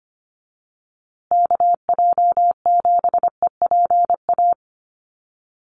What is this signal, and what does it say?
Morse code, heard as a keyed tone, about 25 wpm, 700 Hz. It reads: KJ7EPA
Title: USA Call Signs Repeated 2x 25wpm